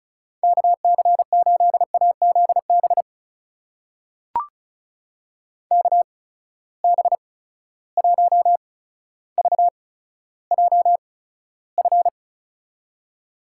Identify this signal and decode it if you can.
Morse code, heard as a keyed tone, about 36 wpm, spaced as for 15 wpm, 700 Hz. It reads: KC8AZB KB1VJF
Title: USA Call Signs - 6 Characters - 8x Character Spacing - 35wpm